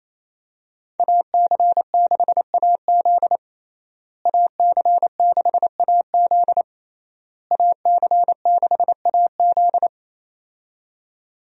Morse code, heard as a keyed tone, about 28 wpm, 700 Hz. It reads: AC6AZ AC6AZ AC6AZ
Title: USA Call Signs - 5 Characters - Repeated 3x - 28wpm